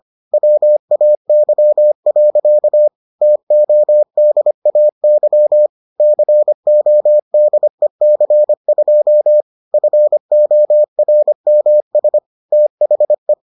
Morse code, heard as a keyed tone, about 25 wpm, 600 Hz. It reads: WAY. TODAY CODEC2 FORMS THE